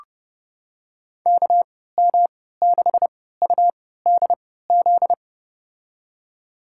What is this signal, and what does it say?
Morse code, heard as a keyed tone, about 30 wpm, spaced as for 17 wpm, 700 Hz. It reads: KM6UDZ
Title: USA Call Signs - 6 Characters - 3x Character Spacing - 30wpm